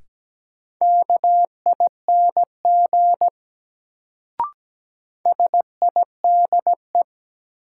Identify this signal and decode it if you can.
Morse code, heard as a keyed tone, about 17 wpm, 700 Hz. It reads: KING SIDE
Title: Sentences from Top 400 Words - New Word Drill - 17wpm